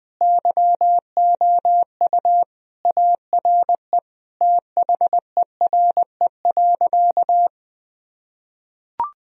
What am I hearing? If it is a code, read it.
Morse code, heard as a keyed tone, about 20 wpm, 700 Hz. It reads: YOU ARE THERE.
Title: Sentences from Top 1000 Words - Review Part 1 - 20wpm